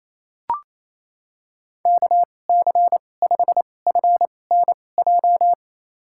Morse code, heard as a keyed tone, about 28 wpm, spaced as for 20 wpm, 700 Hz. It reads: KC5FNJ
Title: USA Call Signs - 2x Character Spacing - 28wpm